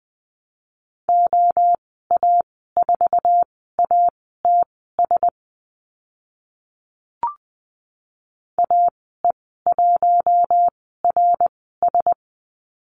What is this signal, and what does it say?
Morse code, heard as a keyed tone, about 20 wpm, spaced as for 15 wpm, 700 Hz. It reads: OA4ATS AE1RS
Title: 069 Lesson 12g - Letter d - Call Signs 20wpm